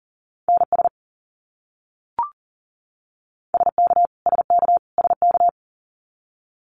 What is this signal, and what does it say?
Morse code, heard as a keyed tone, about 40 wpm, 700 Hz. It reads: NS SK SK SK
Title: Canadian Provinces and Territories - Repeated 3x - 40wpm